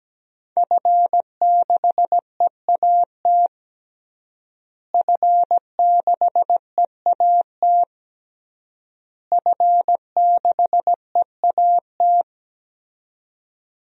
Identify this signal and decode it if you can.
Morse code, heard as a keyed tone, about 17 wpm, 700 Hz. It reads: F6EAT F6EAT F6EAT
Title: DXCC-100 Call Signs - 5 Characters - Repeated 3x - 17wpm